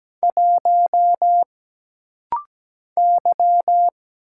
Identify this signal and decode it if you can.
Morse code, heard as a keyed tone, about 17 wpm, 700 Hz. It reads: 1 Y
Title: Single Letter-Number Alt - Rapid-Fire 17wpm